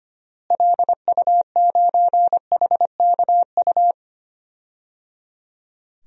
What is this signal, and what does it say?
Morse code, heard as a keyed tone, about 25 wpm, 700 Hz. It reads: LU9HKU
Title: DXCC-100 Call Signs - 6 Characters - Repeated 3x - 25wpm